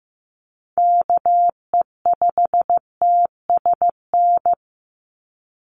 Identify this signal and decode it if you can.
Morse code, heard as a keyed tone, about 15 wpm, 700 Hz. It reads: KE5TSN
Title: USA Call Signs 15wpm